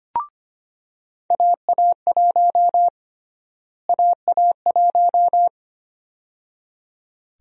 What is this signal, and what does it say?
Morse code, heard as a keyed tone, about 25 wpm, 700 Hz. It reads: AA1 AA1
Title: USA Call Sign Prefixes - Repeated 2x - 25wpm